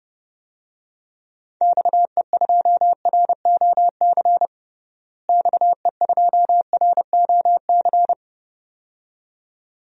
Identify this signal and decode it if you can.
Morse code, heard as a keyed tone, about 30 wpm, 700 Hz. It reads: XE2ROC XE2ROC
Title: DXCC-100 Call Signs - 6 Characters - Repeated 2x - 30wpm